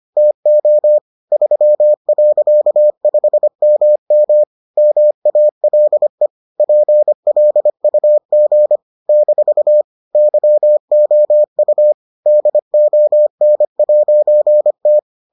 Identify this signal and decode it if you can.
Morse code, heard as a keyed tone, about 25 wpm, 600 Hz. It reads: TO 3.5MM MALE PLUG - YOU DON'T